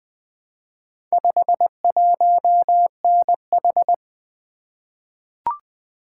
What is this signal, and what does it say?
Morse code, heard as a keyed tone, about 20 wpm, 700 Hz. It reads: 51NH
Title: Four Letter-Number Combinations - 20wpm